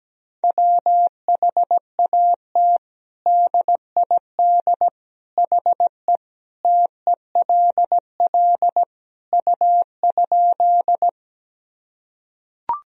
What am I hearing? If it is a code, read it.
Morse code, heard as a keyed tone, about 17 wpm, 700 Hz. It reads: WHAT DID HE TELL U?
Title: Sentences from Top 200 Words and Abbreviations 17wpm